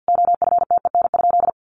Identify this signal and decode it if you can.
Morse code, heard as a keyed tone, about 50 wpm, 700 Hz. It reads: OFTEN?
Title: Sentences from Top 400 Words 50wpm